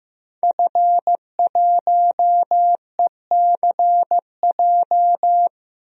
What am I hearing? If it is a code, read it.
Morse code, heard as a keyed tone, about 15 wpm, 700 Hz. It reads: F1ECJ
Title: DXCC-100 Call Signs - 5 Characters - 15wpm